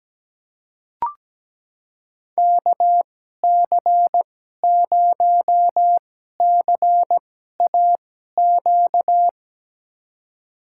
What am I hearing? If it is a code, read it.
Morse code, heard as a keyed tone, about 17 wpm, spaced as for 12 wpm, 700 Hz. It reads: KC0CAQ
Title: USA Call Signs - 6 Characters - 2x Character Spacing - 17wpm